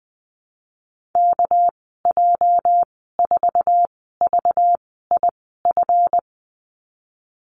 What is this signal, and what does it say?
Morse code, heard as a keyed tone, about 20 wpm, spaced as for 14 wpm, 700 Hz. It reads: KJ4VIF
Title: 245 Lesson 39g - Letter z - Call Signs 20wpm